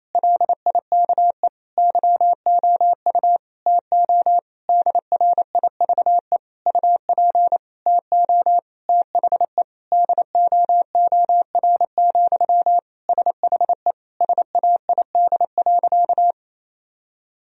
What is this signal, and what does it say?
Morse code, heard as a keyed tone, about 27 wpm, 700 Hz. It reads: LIKE YOU TO DRIVE UP TO THE DOOR, SHE SAID.
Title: Sentences from Top 1000 Words - Review Part 2 - 28wpm